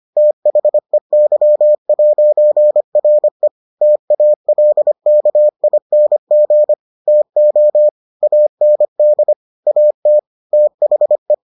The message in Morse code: THEY'RE TALKING TO AND AT THE

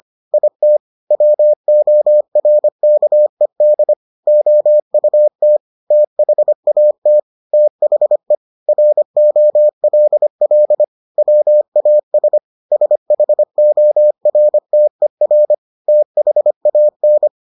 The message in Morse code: IT WORKED OUT THAT THE ROLL WAS SHORTER THAN

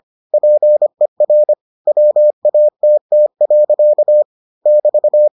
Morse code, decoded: PER WATT. <BT>

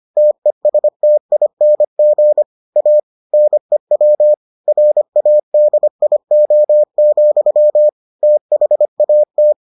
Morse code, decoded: TESTING A NEW RADIO, THAT

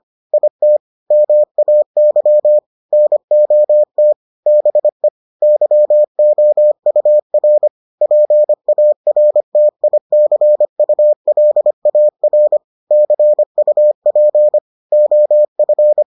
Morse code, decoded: IT MAY NOT BE YOUR PARTICULAR CUP OF

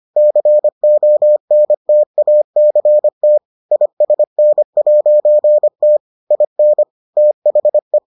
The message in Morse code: CONTACT ISN'T IN THE

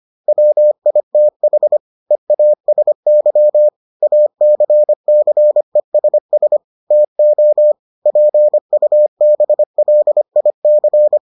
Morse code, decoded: WITH EASY ACCESS TO PUBLIC

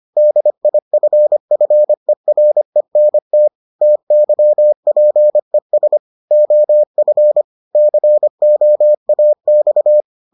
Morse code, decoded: DIFFERENT TYPES OF COAX

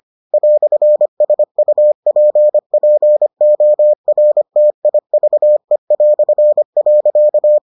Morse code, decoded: "SUPPORTIVE".

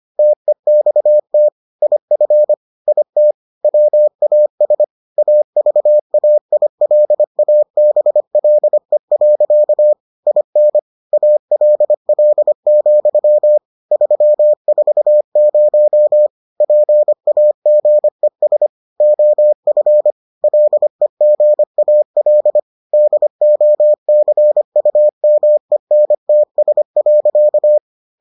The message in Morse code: TEXT IF IT WAS AVAILABLE. IN ALL, 340 PAGES OF LEGAL DOCUMENTS.

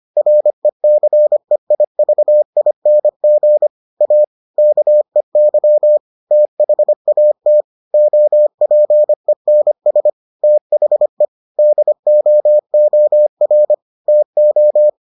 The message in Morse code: RECEIVING A KEY THAT OPENS THE DOOR TO

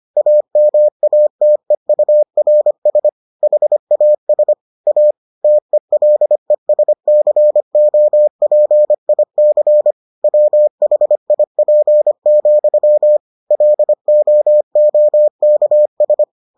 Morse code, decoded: AMATEURS HAS A TELESCOPIC WHIP, LOOKS

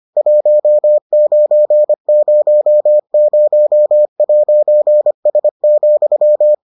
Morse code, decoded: 1900'S,

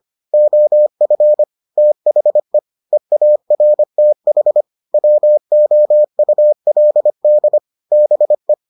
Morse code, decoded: OF THE EARTH WOULD BE

